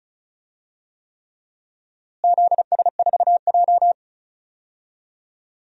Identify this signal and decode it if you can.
Morse code, heard as a keyed tone, about 36 wpm, 700 Hz. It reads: ZS4J
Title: DXCC-100 Call Signs - 4 Characters - Repeated 3x - 35wpm